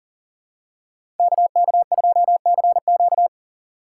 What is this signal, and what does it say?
Morse code, heard as a keyed tone, about 40 wpm, 700 Hz. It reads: KK2CQ
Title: USA Call Signs - 5 Characters - 2x Character Spacing - 40wpm